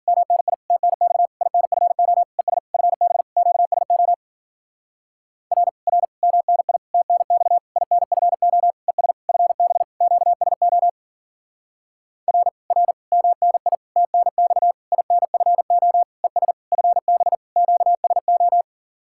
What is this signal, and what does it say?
Morse code, heard as a keyed tone, about 50 wpm, 700 Hz. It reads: MNI TNX INFO ES FB QSO R R MNI TNX INFO ES FB QSO R R MNI TNX INFO ES FB QSO